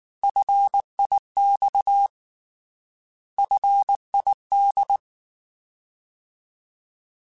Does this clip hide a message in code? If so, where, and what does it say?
Yes: Morse code from 0.2 to 5.0 s, heard as a keyed tone, about 19 wpm, 800 Hz. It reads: FIX FID